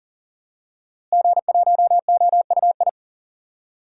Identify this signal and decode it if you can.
Morse code, heard as a keyed tone, about 40 wpm, 700 Hz. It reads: G1OUI
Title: DXCC-100 Call Signs - 5 Characters - Repeated 2x - 40wpm